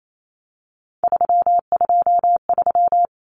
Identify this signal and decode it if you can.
Morse code, heard as a keyed tone, about 28 wpm, 700 Hz. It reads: 323
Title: Common Cut Numbers - 3-Digit 28wpm